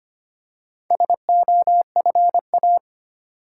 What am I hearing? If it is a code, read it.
Morse code, heard as a keyed tone, about 25 wpm, 690 Hz. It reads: SOFA
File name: practice / 25wpm / Easy Words with Hints - 25wpm